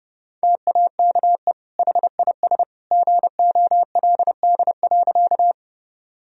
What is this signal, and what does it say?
Morse code, heard as a keyed tone, about 30 wpm, 700 Hz. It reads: TAKE HIS GOLD.
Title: Sentences from Top 1000 Words - Review Part 3 - 30wpm